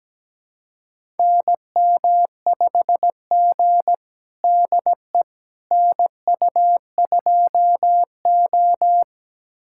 Morse code, heard as a keyed tone, about 17 wpm, 700 Hz. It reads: NM5G DE NU2O
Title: USA Callsign DE Callsign - 4 Characters - 17wpm